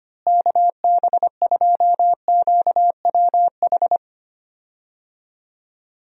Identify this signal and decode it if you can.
Morse code, heard as a keyed tone, about 25 wpm, 700 Hz. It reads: KB2QWH
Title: USA Call Signs - 6 Characters - Repeated 3x - 25wpm